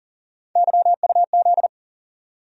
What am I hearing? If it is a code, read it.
Morse code, heard as a keyed tone, about 40 wpm, 700 Hz. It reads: YUZ
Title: USA Call Sign Postfixes - 3 Letters 40wpm